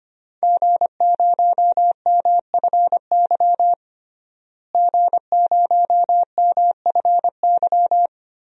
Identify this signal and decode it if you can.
Morse code, heard as a keyed tone, about 25 wpm, 700 Hz. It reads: G0MFY G0MFY
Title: DXCC-100 Call Signs - 5 Characters - Repeated 3x - 25wpm